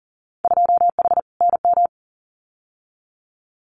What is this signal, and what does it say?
Morse code, heard as a keyed tone, about 41 wpm, 700 Hz. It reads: 2H NM
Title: Winter Field Day Exchanges Encore 40wpm